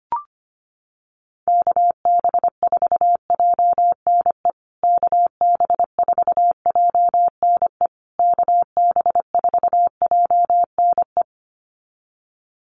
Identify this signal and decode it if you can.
Morse code, heard as a keyed tone, about 25 wpm, 700 Hz. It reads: KB4JNE KB4JNE KB4JNE
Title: USA Call Signs Repeated 3x 25wpm